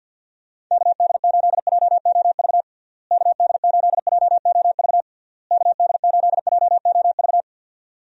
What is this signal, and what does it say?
Morse code, heard as a keyed tone, about 52 wpm, 700 Hz. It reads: KD8JOV KD8JOV KD8JOV
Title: USA Call Signs - 6 Characters - Repeated 3x - 50wpm